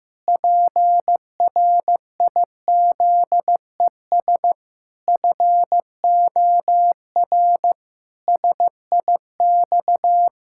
Morse code, heard as a keyed tone, about 15 wpm, 700 Hz. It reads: PRIZES FOR SIX
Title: Pangram Challenge v2 - 15wpm